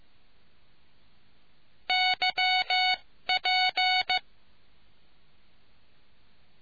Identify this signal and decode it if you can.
Morse code, heard as a keyed tone, about 15 wpm, spaced as for 13 wpm, 740 Hz. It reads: YP